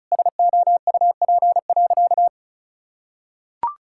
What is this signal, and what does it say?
Morse code, heard as a keyed tone, about 36 wpm, 700 Hz. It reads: SOUP.
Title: Sentences from Top 1000 Words - Review Part 6 - 35wpm